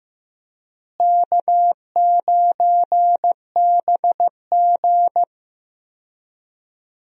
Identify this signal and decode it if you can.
Morse code, heard as a keyed tone, about 15 wpm, 700 Hz. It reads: K9BG
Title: USA Call Signs - 4 Characters - Repeated 2x - 15wpm